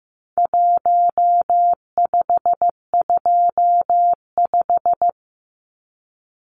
Morse code, heard as a keyed tone, about 15 wpm, 700 Hz. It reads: 1525
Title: Numbers - Contest Style - 4-Digit 15wpm